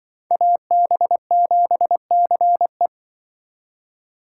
Morse code, heard as a keyed tone, about 24 wpm, 700 Hz. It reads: AB7CE
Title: Speed Racing - USA 2x2 Call Signs 20wpm